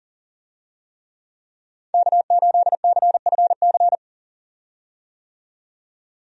Morse code, heard as a keyed tone, about 38 wpm, 700 Hz. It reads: K8CFC
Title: USA Call Signs - 5 Characters - 40wpm